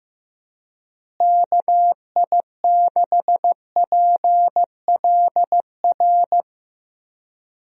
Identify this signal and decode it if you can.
Morse code, heard as a keyed tone, about 16 wpm, 700 Hz. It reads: KI6PLR
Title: USA Call Signs - 6 Characters - Repeated 2x - 15wpm